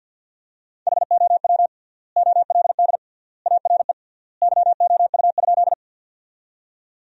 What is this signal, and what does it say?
Morse code, heard as a keyed tone, about 52 wpm, 700 Hz. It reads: HOW OLD ARE YOU?
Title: 3x Word-Spacing - Sentences from Top 200 Words - 50wpm